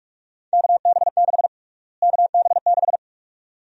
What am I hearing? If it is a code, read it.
Morse code, heard as a keyed tone, about 46 wpm, 700 Hz. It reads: KB6 KB6
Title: USA Call Sign Prefixes - Repeated 2x - 45wpm